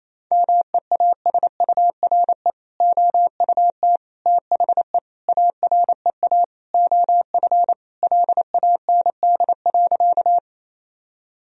Morse code, heard as a keyed tone, about 28 wpm, 700 Hz. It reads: MEASURE OUT THE AREA OF LAND.
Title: Sentences from Top 1000 Words - Review Part 2 - 28wpm